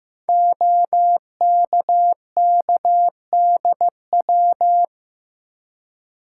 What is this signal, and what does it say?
Morse code, heard as a keyed tone, about 15 wpm, 700 Hz. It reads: OKKDW
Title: Code Groups - 5 Letter:Numbers - 15wpm